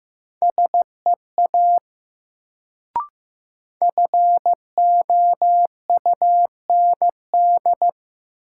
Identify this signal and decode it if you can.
Morse code, heard as a keyed tone, about 15 wpm, 700 Hz. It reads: SEA FOUND
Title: Sentences from Top 300 Words - New Word Drill - 15wpm